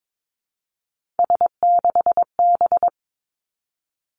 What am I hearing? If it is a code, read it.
Morse code, heard as a keyed tone, about 22 wpm, 700 Hz. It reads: S6B
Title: Three Challenging Letter-Number Combinations 22wpm